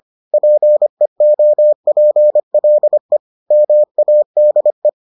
Morse code, heard as a keyed tone, about 25 wpm, 600 Hz. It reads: PEOPLE MADE